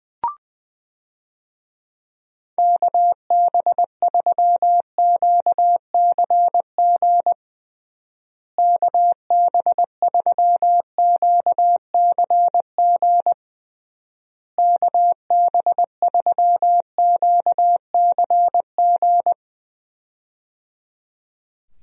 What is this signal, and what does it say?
Morse code, heard as a keyed tone, about 20 wpm, 700 Hz. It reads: KB3QCG KB3QCG KB3QCG
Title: USA Call Signs - 6 Characters - Repeated 3x - 20wpm